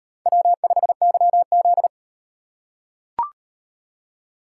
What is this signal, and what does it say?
Morse code, heard as a keyed tone, about 34 wpm, 700 Hz. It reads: W5YZ